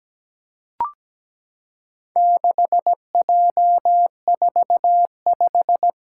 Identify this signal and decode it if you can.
Morse code, heard as a keyed tone, about 17 wpm, 700 Hz. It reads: 6J45
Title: Four Challenging Letter-Number Combinations 17wpm